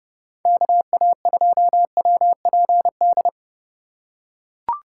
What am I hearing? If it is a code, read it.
Morse code, heard as a keyed tone, about 32 wpm, 700 Hz. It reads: KA2WPD
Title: USA Call Signs - 2x Character Spacing - 30wpm